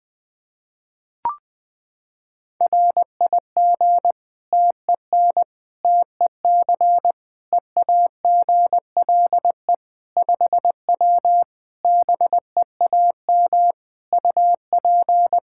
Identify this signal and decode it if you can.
Morse code, heard as a keyed tone, about 20 wpm, 700 Hz. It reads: RIG TEN TEC EAGLE 5W BEAM UP
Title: Practice QSO - Exchange 2 - Info - 20wpm